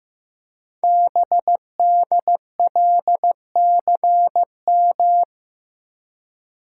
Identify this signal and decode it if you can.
Morse code, heard as a keyed tone, about 15 wpm, 700 Hz. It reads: BDLCM
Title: Code Groups - 5 Letters - 15wpm